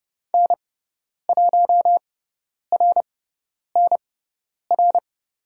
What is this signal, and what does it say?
Morse code, heard as a keyed tone, about 30 wpm, spaced as for 10 wpm, 700 Hz. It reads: N1RNR
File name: Lesson 16g - Letter c - Call Signs 30wpm